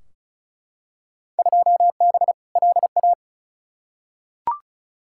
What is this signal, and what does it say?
Morse code, heard as a keyed tone, about 36 wpm, 700 Hz. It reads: 2B LA
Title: Field Day Exchanges Encore - 35wpm